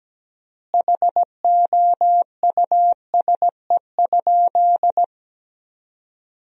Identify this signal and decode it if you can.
Morse code, heard as a keyed tone, about 17 wpm, 700 Hz. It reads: HOUSE?
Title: 3x Word-Spacing - Sentences from Top 200 Words - 17wpm